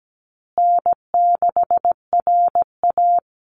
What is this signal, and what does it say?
Morse code, heard as a keyed tone, about 17 wpm, 700 Hz. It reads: N6RA
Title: Easy Call Signs Repeated 3x - 17wpm